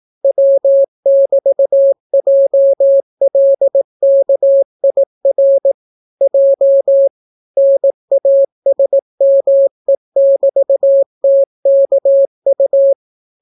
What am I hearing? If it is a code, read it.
Morse code, heard as a keyed tone, about 18 wpm, 550 Hz. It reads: W<BT>JLKIR J NASME<BT>TKU